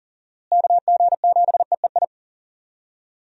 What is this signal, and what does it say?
Morse code, heard as a keyed tone, about 37 wpm, 700 Hz. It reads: KG7EEI
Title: USA Call Signs - 6 Characters - Repeated 2x - 40wpm